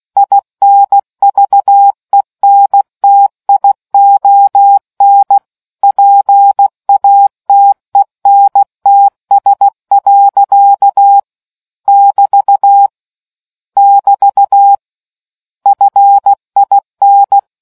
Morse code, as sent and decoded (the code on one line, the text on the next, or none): .. -. ...- . -. - .. --- -. / .--. .- - . -. - ... .-.-.- / -...- / -...- / ..-. .. -.
INVENTION PATENTS. <BT> <BT> FIN